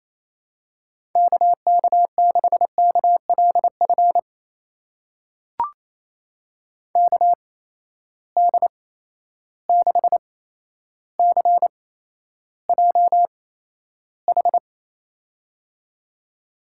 Morse code, heard as a keyed tone, about 28 wpm, spaced as for 12 wpm, 700 Hz. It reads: KK6KLF KD6CJH
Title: USA Call Signs - 8x Character Spacing - 28wpm